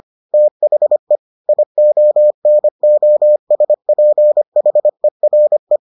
Morse code, heard as a keyed tone, about 25 wpm, 600 Hz. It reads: THE IONOSPHERE